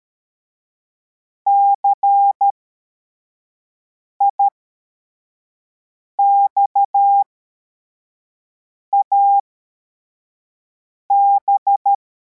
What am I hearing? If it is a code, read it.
Morse code, heard as a keyed tone, about 13 wpm, spaced as for 4 wpm, 800 Hz. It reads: CIXAB
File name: formation / cw / Text52